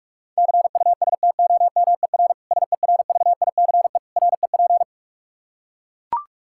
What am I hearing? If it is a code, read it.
Morse code, heard as a keyed tone, about 41 wpm, 700 Hz. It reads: CUSTOMER SERVICE REP
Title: K1USN SST QSO Practice 45wpm